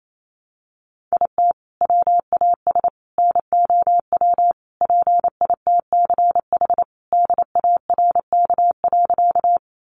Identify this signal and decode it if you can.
Morse code, heard as a keyed tone, about 28 wpm, 700 Hz. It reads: IT WAS NOW PITCH DARK.
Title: Sentences from Top 1300 Words 28wpm